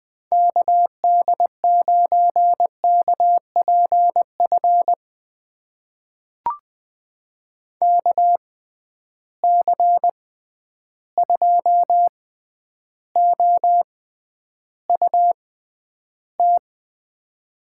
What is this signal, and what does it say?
Morse code, heard as a keyed tone, about 20 wpm, spaced as for 10 wpm, 700 Hz. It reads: KD9KPF KC2OUT
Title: USA Call Signs - 6 Characters - 6x Character Spacing - 20wpm